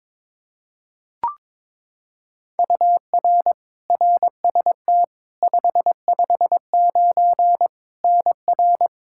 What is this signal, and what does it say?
Morse code, heard as a keyed tone, about 22 wpm, 700 Hz. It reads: UR RST 559 NR
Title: Practice QSO - Exchange 1 - Beginning - 22wpm